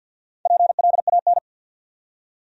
Morse code, heard as a keyed tone, about 64 wpm, 700 Hz. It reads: PLAN